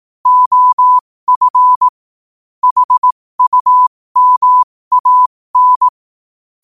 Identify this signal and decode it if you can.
Morse code, heard as a keyed tone, about 18 wpm, spaced as for 15 wpm, 1000 Hz. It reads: OF HUMAN
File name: qotdfiles / 15wpm